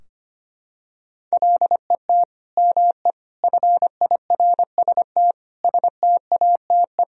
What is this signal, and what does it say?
Morse code, heard as a keyed tone, about 25 wpm, 700 Hz. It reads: LET ME FIRST STATE
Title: Sentences from Top 1000 Words - Review Part 2 - 25wpm